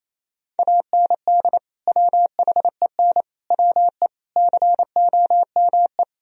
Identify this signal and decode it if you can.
Morse code, heard as a keyed tone, about 34 wpm, 700 Hz. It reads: AND WHEN WE COME
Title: Speed-Racing - Sets of 5 Words 28wpm